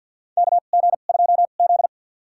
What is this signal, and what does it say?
Morse code, heard as a keyed tone, about 53 wpm, spaced as for 37 wpm, 700 Hz. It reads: KG2Z
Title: USA Call Signs - 2x Character Spacing - 50wpm